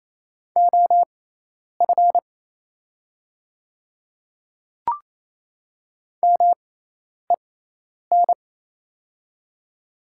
Morse code, heard as a keyed tone, about 27 wpm, spaced as for 10 wpm, 700 Hz. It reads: OF MEN
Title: 6x Character Spacing - Top 200 Words - 28wpm